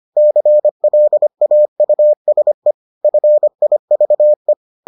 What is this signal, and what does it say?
Morse code, heard as a keyed tone, about 25 wpm, 600 Hz. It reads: CLAUSE FIVE